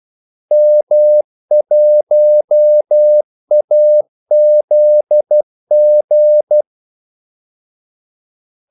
Morse code, heard as a keyed tone, about 12 wpm, 600 Hz. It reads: M1AZG
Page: 2 ekstra ordmellemrum mellem kaldesignalerne